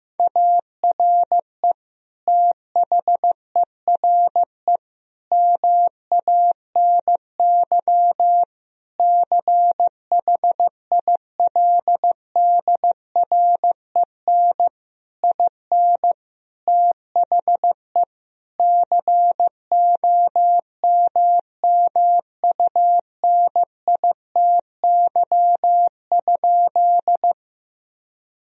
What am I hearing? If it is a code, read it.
Morse code, heard as a keyed tone, about 15 wpm, 700 Hz. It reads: ARE THERE MANY CHILDREN IN THE COMMUNITY?
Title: Sentences from Top 1000 Words - Review Part 4 - 15wpm